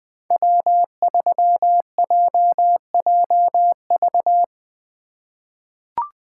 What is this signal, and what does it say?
Morse code, heard as a keyed tone, about 20 wpm, 700 Hz. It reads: W3JJV
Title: USA Call Signs - 5 Characters - 8x Character Spacing - 20wpm